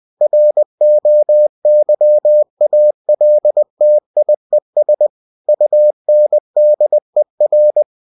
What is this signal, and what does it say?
Morse code, heard as a keyed tone, about 20 wpm, 600 Hz. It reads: ROYALTIES UNDER
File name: cw-books / winnie-the-pooh / winnie-20wpm-page-0117